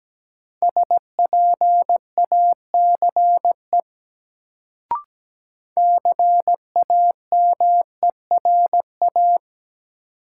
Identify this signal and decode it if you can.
Morse code, heard as a keyed tone, about 17 wpm, 700 Hz. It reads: SPACE CAMERA
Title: Sentences from Top 1000 Words - Review Part 5 - New Word Drill - 17wpm